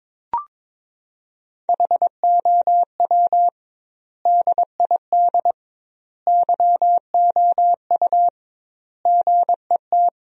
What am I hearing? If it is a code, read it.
Morse code, heard as a keyed tone, about 22 wpm, 700 Hz. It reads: HOW DID YOU GET
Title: 2x Word-Spacing - Sentences from Top 200 Words - 22wpm